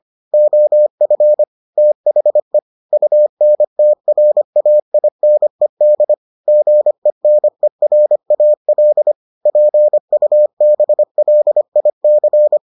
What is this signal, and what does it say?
Morse code, heard as a keyed tone, about 25 wpm, 600 Hz. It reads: OF THE UNTRAINED GENERAL PUBLIC